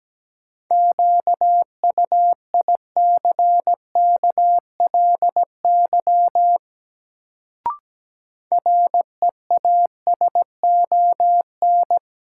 Morse code, heard as a keyed tone, about 17 wpm, 700 Hz. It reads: QUICKLY REASON
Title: Sentences from Top 700 Words - New Word Drill - 17wpm